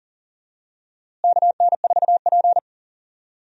Morse code, heard as a keyed tone, about 40 wpm, 700 Hz. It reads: KN4P
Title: USA Call Signs - 4 Characters - Repeated 3x - 40wpm